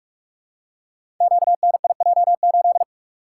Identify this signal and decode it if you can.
Morse code, heard as a keyed tone, about 44 wpm, 700 Hz. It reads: QNIJ8
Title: Code Groups - 5 Letter:Numbers - 45wpm